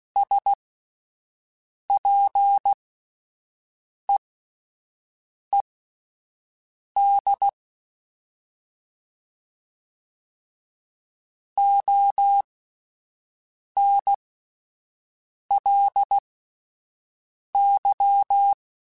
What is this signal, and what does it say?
Morse code, heard as a keyed tone, about 16 wpm, spaced as for 5 wpm, 800 Hz. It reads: SPEED ONLY
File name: Spaced Characters / Spaced Characters-mp3s / Weather 5 1 sp @10